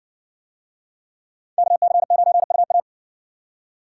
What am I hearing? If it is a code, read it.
Morse code, heard as a keyed tone, about 56 wpm, 700 Hz. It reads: KC9UA